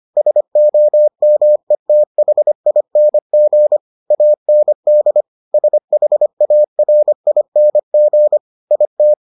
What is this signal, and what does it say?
Morse code, heard as a keyed tone, about 25 wpm, 600 Hz. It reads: SOMETHING AND SHARING IT